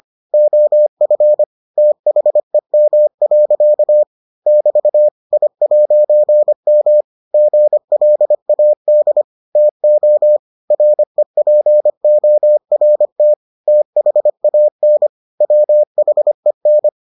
Morse code, heard as a keyed tone, about 25 wpm, 600 Hz. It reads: OF THEM. <BT> I'M GLAD TO REPORT THAN WHEN